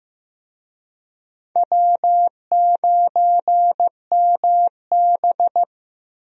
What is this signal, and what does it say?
Morse code, heard as a keyed tone, about 15 wpm, 700 Hz. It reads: W9MB
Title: USA Call Signs - 4 Characters - Repeated 3x - 15wpm